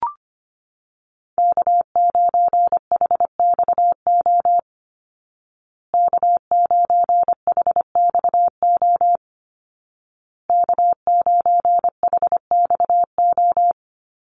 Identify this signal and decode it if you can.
Morse code, heard as a keyed tone, about 25 wpm, 700 Hz. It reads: K9HXO K9HXO K9HXO
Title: Call Signs Repeated 2x - 25wpm